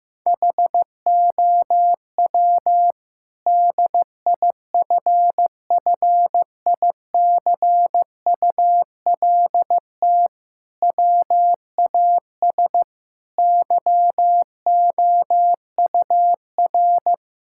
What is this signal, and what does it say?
Morse code, heard as a keyed tone, about 15 wpm, 700 Hz. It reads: HOW DIFFICULT WAS YOUR
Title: Sentences from Top 1000 Words - Review Part 6 - 15wpm